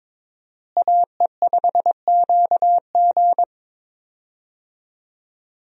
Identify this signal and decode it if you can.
Morse code, heard as a keyed tone, about 22 wpm, 700 Hz. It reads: AE5QG
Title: USA Call Signs - 5 Characters - Repeated 3x - 22wpm